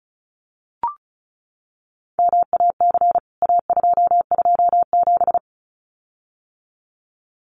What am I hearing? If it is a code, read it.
Morse code, heard as a keyed tone, about 34 wpm, 700 Hz. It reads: MAC A227
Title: CWT Exchange 35wpm